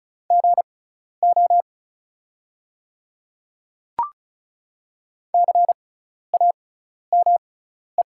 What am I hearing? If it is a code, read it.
Morse code, heard as a keyed tone, about 34 wpm, spaced as for 12 wpm, 700 Hz. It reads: GO CAME
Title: 6x Character Spacing - Top 200 Words - 35wpm